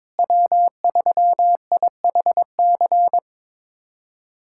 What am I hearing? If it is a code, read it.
Morse code, heard as a keyed tone, about 22 wpm, 700 Hz. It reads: W3IHC
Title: USA Callsign DE Callsign 22wpm